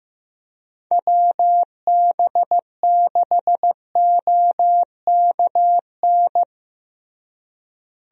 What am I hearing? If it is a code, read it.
Morse code, heard as a keyed tone, about 15 wpm, 700 Hz. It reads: WB6OKN
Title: USA Call Signs - 6 Characters - Repeated 3x - 15wpm